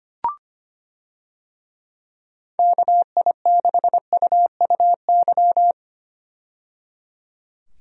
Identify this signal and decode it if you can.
Morse code, heard as a keyed tone, about 25 wpm, 700 Hz. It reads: KI6UUY
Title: USA Call Signs - 6 Characters - 25wpm